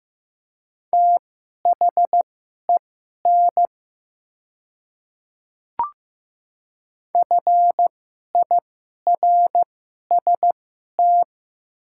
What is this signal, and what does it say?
Morse code, heard as a keyed tone, about 15 wpm, spaced as for 11 wpm, 700 Hz. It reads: THEN FIRST
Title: 2x Character Spacing - Top 100 Words - 15wpm